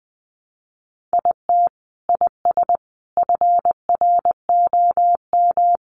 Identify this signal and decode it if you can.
Morse code, heard as a keyed tone, about 20 wpm, 700 Hz. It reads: IT IS FROM
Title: Sentences from Top 300 Words 20wpm